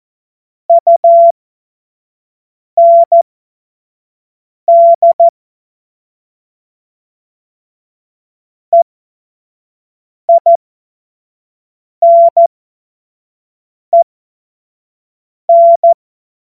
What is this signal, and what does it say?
Morse code, heard as a keyed tone, about 14 wpm, spaced as for 5 wpm, 680 Hz. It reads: UND EINEN
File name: Morsekurs / Witze / Witz 02. Mai 2023 25 F70